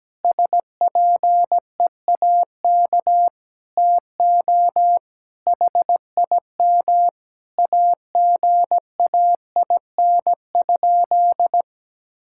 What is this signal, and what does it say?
Morse code, heard as a keyed tone, about 17 wpm, 700 Hz. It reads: SPEAK TO HIM AGAIN?
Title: Sentences from Top 1600 Words 17wpm